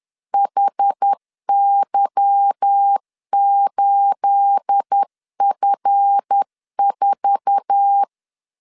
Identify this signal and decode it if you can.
Morse code, heard as a keyed tone, about 10 wpm, 800 Hz. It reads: HY8F4
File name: LEZIONI / 9 28 10wpm 2842